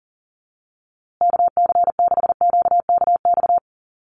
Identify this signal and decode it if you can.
Morse code, heard as a keyed tone, about 40 wpm, 700 Hz. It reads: KC6QKX
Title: USA Call Signs Repeated 3x 40wpm